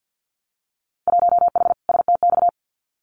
Morse code, heard as a keyed tone, about 50 wpm, 700 Hz. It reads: 1H STX
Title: Winter Field Day Exchanges 50wpm